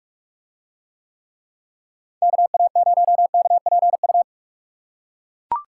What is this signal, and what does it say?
Morse code, heard as a keyed tone, about 42 wpm, 700 Hz. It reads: KA0KPU